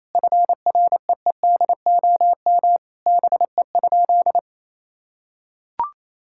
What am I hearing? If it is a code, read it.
Morse code, heard as a keyed tone, about 28 wpm, 700 Hz. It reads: FREEDOM BE?
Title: Sentences from Top 1000 Words - Review Part 5 - 28wpm